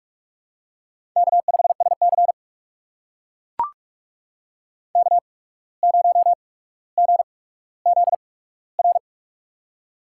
Morse code, heard as a keyed tone, about 44 wpm, spaced as for 18 wpm, 700 Hz. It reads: K5SC K0GZR
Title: USA Call Signs - 8x Character Spacing - 45wpm